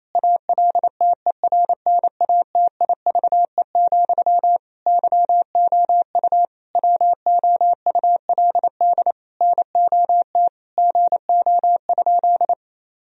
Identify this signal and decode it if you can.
Morse code, heard as a keyed tone, about 28 wpm, 700 Hz. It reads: ALTERNATIVE, YOU WOULD NOT GO?
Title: Sentences from Top 1000 Words - Review Part 4 - 28wpm